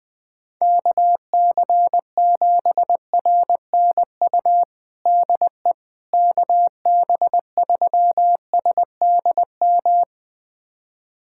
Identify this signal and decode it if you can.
Morse code, heard as a keyed tone, about 20 wpm, 700 Hz. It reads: KC7RNU DE KB3SDM
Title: USA Callsign DE Callsign - 6 Characters - 20wpm